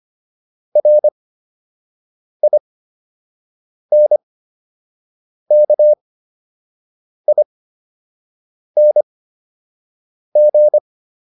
Morse code, heard as a keyed tone, about 25 wpm, spaced as for 6 wpm, 600 Hz. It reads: RINKING